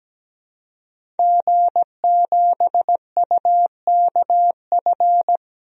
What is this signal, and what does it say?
Morse code, heard as a keyed tone, about 17 wpm, 700 Hz. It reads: G7UKF
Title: DXCC-100 Call Signs - 5 Characters - Repeated 2x - 17wpm